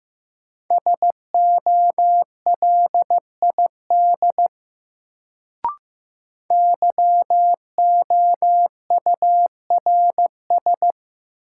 Morse code, heard as a keyed tone, about 15 wpm, 700 Hz. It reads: SOLID YOURS
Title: Sentences from Top 1500 Words - New Word Drill - 15wpm